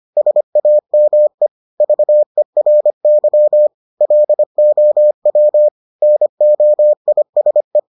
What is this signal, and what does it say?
Morse code, heard as a keyed tone, about 25 wpm, 600 Hz. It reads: SAME VERY LOW NOISE